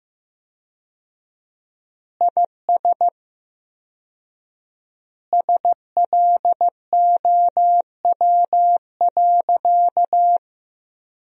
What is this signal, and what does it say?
Morse code, heard as a keyed tone, about 15 wpm, 700 Hz. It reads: IS SLOW.
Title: 4x Word-Spacing - Sentences from Top 400 Words - 15wpm